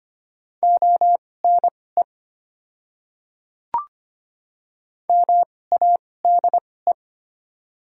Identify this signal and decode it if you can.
Morse code, heard as a keyed tone, about 25 wpm, spaced as for 18 wpm, 690 Hz. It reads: ONE MADE
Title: 2x Character Spacing - Top 200 Words - 25wpm